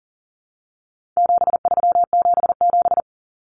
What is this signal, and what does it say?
Morse code, heard as a keyed tone, about 40 wpm, 700 Hz. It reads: 7377
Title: Common Cut Numbers - 4-Digit 40wpm